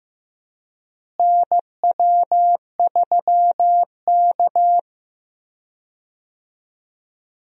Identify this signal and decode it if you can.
Morse code, heard as a keyed tone, about 16 wpm, 700 Hz. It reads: NW3K
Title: USA Call Signs - 4 Characters - Repeated 2x - 15wpm